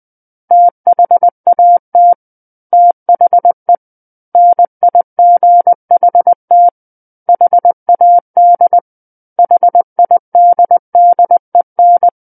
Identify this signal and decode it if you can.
Morse code, heard as a keyed tone, about 20 wpm, 700 Hz. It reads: THAT THE NIGHT HAD HIDDEN
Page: SKCC Learning Center - Morse Code Practice Files